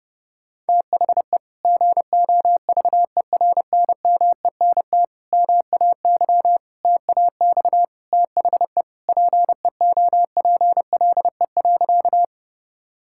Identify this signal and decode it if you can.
Morse code, heard as a keyed tone, about 30 wpm, 700 Hz. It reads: THE GOVERNMENT MAY TAX THE PEOPLE.
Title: Sentences from Top 1000 Words - Review Part 5 - 30wpm